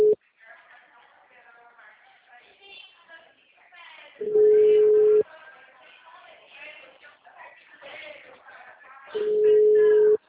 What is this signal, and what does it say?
Ringback tone (Europe, 425 Hz), with speech.